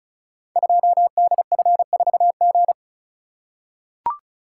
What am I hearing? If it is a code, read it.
Morse code, heard as a keyed tone, about 36 wpm, 700 Hz. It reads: 2DF4G
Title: Code Groups - 5 Letter:Numbers - 35wpm